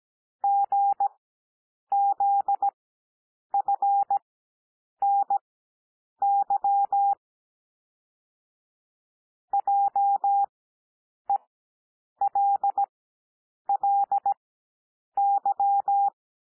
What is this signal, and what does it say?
Morse code, heard as a keyed tone, about 17 wpm, spaced as for 8 wpm, 800 Hz. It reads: GZFNY JELLY
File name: cwmp3 / L2 SOUND LETTERS 17 wpm